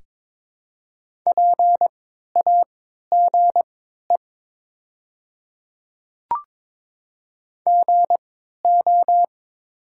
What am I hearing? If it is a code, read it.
Morse code, heard as a keyed tone, about 22 wpm, spaced as for 12 wpm, 690 Hz. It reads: PAGE GO
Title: 3x Character Spacing - Top 300 Words - 22wpm